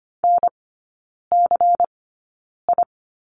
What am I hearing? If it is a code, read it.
Morse code, heard as a keyed tone, about 25 wpm, spaced as for 9 wpm, 700 Hz. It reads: NCI
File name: course / 25:10wpm / 093 Lesson 16e - Letter c - Three Characters 25wpm